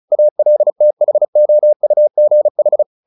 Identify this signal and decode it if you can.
Morse code, heard as a keyed tone, about 34 wpm, 600 Hz. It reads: ALTHOUGH
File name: the old man and the sea 2 35wpm